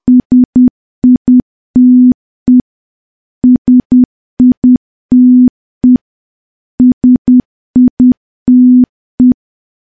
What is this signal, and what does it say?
Morse code, heard as a keyed tone, about 10 wpm, 250 Hz. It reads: SITE SITE SITE
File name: morse / morse-training-10WPM